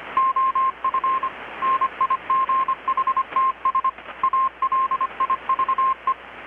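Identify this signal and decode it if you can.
Morse code, heard as a keyed tone, about 25 wpm, 1060 Hz. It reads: OF NIGHTS ALIVE
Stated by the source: Morse from KPH's Night of Nights XX maritime radiotelegraphy event, received in Chiapas, Mexico about 3,800 km away on a Kenwood R-600 with a grounded 10 m long-wire antenna.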